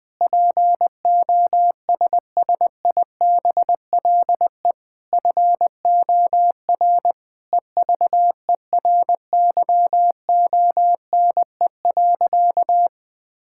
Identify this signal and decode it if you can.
Morse code, heard as a keyed tone, about 21 wpm, 700 Hz. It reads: POSSIBLE FOR EVERYONE.
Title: Sentences with 8-Word Phrases - 20wpm